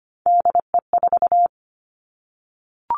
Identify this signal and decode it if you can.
Morse code, heard as a keyed tone, about 25 wpm, 700 Hz. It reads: DE4
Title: Three Challenging Letter-Number Combinations 25wpm